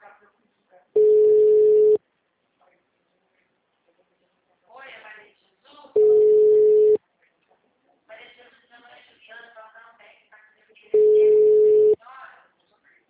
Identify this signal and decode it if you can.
Ringback tone (Europe, 425 Hz), with speech.
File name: out- -2002- - - .47281